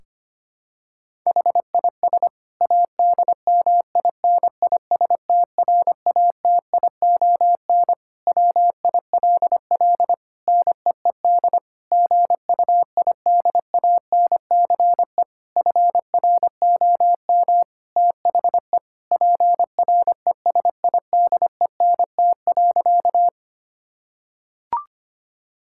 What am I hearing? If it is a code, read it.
Morse code, heard as a keyed tone, about 25 wpm, 700 Hz. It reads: HIS ADMINISTRATION WILL NEED GUIDANCE FROM THE PRESIDENT.
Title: Sentences from Top 1600 Words 25wpm